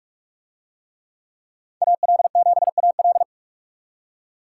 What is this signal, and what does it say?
Morse code, heard as a keyed tone, about 42 wpm, 700 Hz. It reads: AL7AL